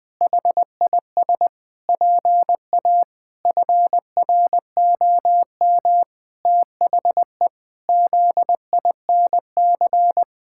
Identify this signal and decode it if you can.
Morse code, heard as a keyed tone, about 21 wpm, 700 Hz. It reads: HIS PA FROM THE ZINC
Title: Pangram Challenge v2 - 20wpm